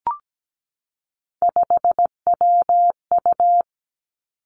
Morse code, heard as a keyed tone, about 17 wpm, 700 Hz. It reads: 5WU
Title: Three Challenging Letter-Number Combinations 17wpm